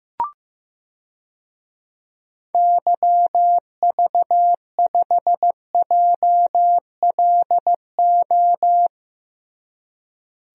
Morse code, heard as a keyed tone, about 15 wpm, 700 Hz. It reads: YV5JLO
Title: DXCC-100 Call Signs - 6 Characters - Repeated 3x - 15wpm